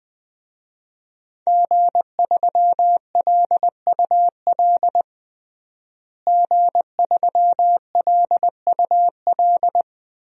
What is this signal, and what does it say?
Morse code, heard as a keyed tone, about 20 wpm, 700 Hz. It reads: G3LUL G3LUL
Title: DXCC-100 Call Signs - 5 Characters - Repeated 3x - 20wpm